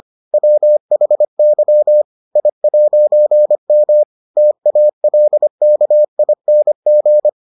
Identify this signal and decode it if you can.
Morse code, heard as a keyed tone, about 25 wpm, 600 Hz. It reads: WHY I'M TALKING